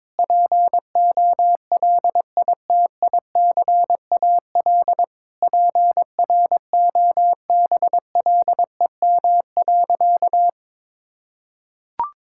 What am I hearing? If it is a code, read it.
Morse code, heard as a keyed tone, about 22 wpm, 700 Hz. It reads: POLITICAL PROBLEM.
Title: Sentences from Top 1000 Words - Review Part 5 - 22wpm